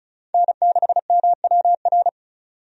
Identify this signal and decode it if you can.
Morse code, heard as a keyed tone, about 36 wpm, 700 Hz. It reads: N6MWR
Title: USA Call Signs - 5 Characters - 35wpm